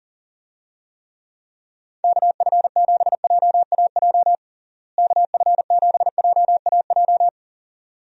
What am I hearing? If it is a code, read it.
Morse code, heard as a keyed tone, about 40 wpm, 700 Hz. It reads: KF7JAJ KF7JAJ
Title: USA Call Signs - 6 Characters - Repeated 2x - 40wpm